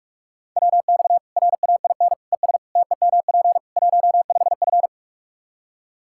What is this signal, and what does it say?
Morse code, heard as a keyed tone, about 42 wpm, 700 Hz. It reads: WX RAIN ES TEMP 15F
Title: Practice QSO - Exchange 2 - Info - 45wpm